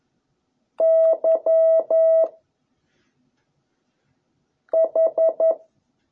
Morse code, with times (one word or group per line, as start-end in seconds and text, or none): 0.8-2.2 Y
4.7-5.5 H